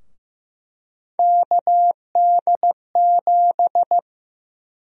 Morse code, heard as a keyed tone, about 15 wpm, 700 Hz. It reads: KD7
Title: USA Call Sign Prefixes - Repeated 2x - 15wpm